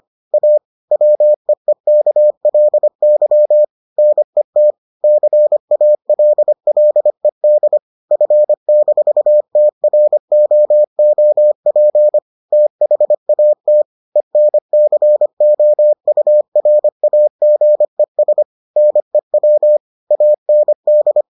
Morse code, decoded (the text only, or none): A WEEKLY NET CALLED F-TROOP THAT ENCOURAGES NEW AND